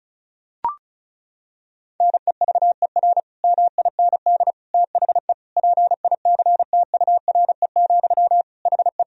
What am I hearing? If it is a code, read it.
Morse code, heard as a keyed tone, about 34 wpm, 700 Hz. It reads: NEVER MIND THE PICTURE, HE
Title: Sentences from Top 1000 Words - Review Part 1 - 35wpm